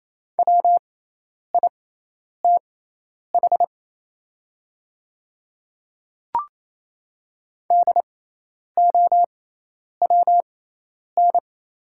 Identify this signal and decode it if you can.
Morse code, heard as a keyed tone, about 28 wpm, spaced as for 10 wpm, 700 Hz. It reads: WITH DOWN
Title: 6x Character Spacing - Top 100 Words - 28wpm